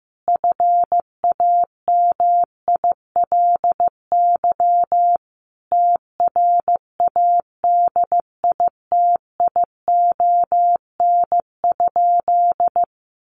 Morse code, heard as a keyed tone, about 15 wpm, 700 Hz. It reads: FAMILY TRADITION?
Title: Sentences from Top 1500 Words 15wpm